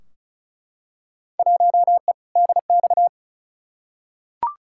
Morse code, heard as a keyed tone, about 35 wpm, 700 Hz. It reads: 1E DX
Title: Field Day Exchanges Encore - 35wpm